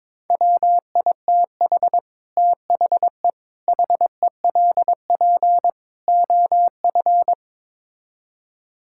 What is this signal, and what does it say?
Morse code, heard as a keyed tone, about 22 wpm, 700 Hz. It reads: WITH THE HELP OF
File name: practice / 22wpm / Build A Phrase - 5-Word Phrases - 22wpm